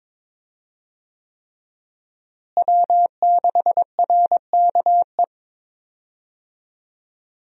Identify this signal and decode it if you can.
Morse code, heard as a keyed tone, about 22 wpm, 700 Hz. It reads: W6RKE
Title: USA Call Signs - 5 Characters - Repeated 3x - 22wpm